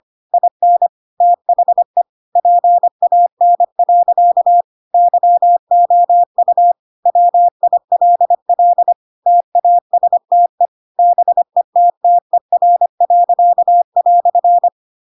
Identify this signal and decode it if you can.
Morse code, heard as a keyed tone, about 25 wpm, 700 Hz. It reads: IN THE PAN. YOU WILL TASTE BETTER."
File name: Pinocchio / Pinocchio ch5 25 25 ik1hge